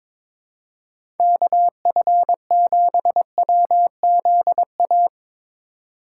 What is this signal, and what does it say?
Morse code, heard as a keyed tone, about 22 wpm, 700 Hz. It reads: KF7WZA
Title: USA Call Signs - 6 Characters - 8x Character Spacing - 22wpm